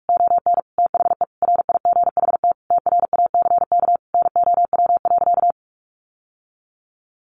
Morse code, heard as a keyed tone, about 46 wpm, 700 Hz. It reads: ON THE RIGHT TRACK NOW.
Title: Sentences from Top 1100 Words 45wpm